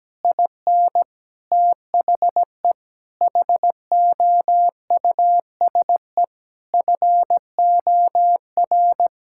Morse code, heard as a keyed tone, about 17 wpm, 700 Hz. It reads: IN THE HOUSE FOR
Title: Sentences from Top 1000 Words - Review Part 5 - 17wpm